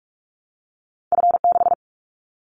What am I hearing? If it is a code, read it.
Morse code, heard as a keyed tone, about 44 wpm, 700 Hz. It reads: F6
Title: Callsign Prefixes for Top 25 DXCC - 45wpm